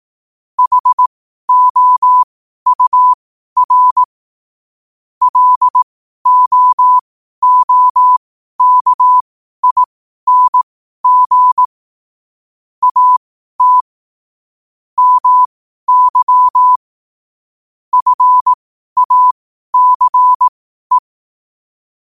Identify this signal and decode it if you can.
Morse code, heard as a keyed tone, about 18 wpm, spaced as for 12 wpm, 1000 Hz. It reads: HOUR LOOKING AT MY FACE
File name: qotdfiles / 12wpm